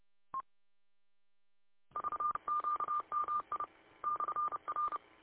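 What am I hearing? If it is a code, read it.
Morse code, heard as a keyed tone, about 30 wpm, 1200 Hz. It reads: VQMI /R
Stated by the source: Morse from W8UM audio archive.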